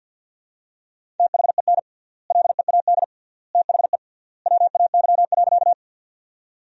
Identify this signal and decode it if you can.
Morse code, heard as a keyed tone, about 50 wpm, 700 Hz. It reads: THEN LEAD THE WAY.
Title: 3x Word-Spacing - Sentences from Top 500 Words - 50wpm